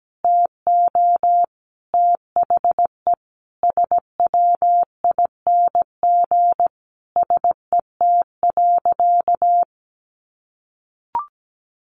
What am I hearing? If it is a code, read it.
Morse code, heard as a keyed tone, about 17 wpm, 700 Hz. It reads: TO THE SWING SET.
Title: Additional Easy Phrases 17wpm